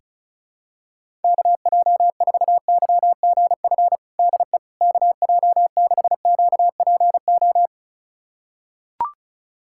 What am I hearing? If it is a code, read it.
Morse code, heard as a keyed tone, about 34 wpm, 700 Hz. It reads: KJ4YGF DE KJ6QPO
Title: USA Callsign DE Callsign 35wpm